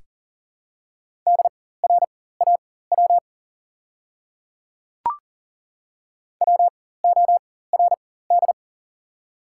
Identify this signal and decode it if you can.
Morse code, heard as a keyed tone, about 40 wpm, spaced as for 19 wpm, 700 Hz. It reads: DRAW WORD
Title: 4x Character Spacing - Top 500 Words - 40wpm